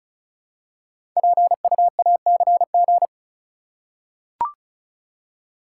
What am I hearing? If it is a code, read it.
Morse code, heard as a keyed tone, about 36 wpm, 700 Hz. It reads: PUACG
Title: Code Groups - 5 Letters - 35wpm